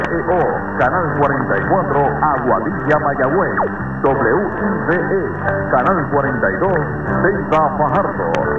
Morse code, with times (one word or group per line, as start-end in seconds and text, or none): none